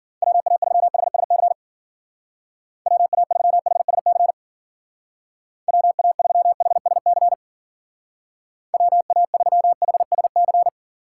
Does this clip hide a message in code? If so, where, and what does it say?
Yes: Morse code from 0.2 to 10.7 s, heard as a keyed tone, about 52 wpm, 700 Hz. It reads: WA3HSC WA3HSC WA3HSC WA3HSC